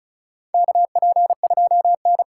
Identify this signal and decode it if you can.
Morse code, heard as a keyed tone, about 36 wpm, 700 Hz. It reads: KP2N
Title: K1USN SST QSO Practice - 35wpm